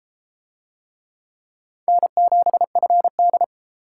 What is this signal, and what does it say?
Morse code, heard as a keyed tone, about 28 wpm, 700 Hz. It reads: N7FD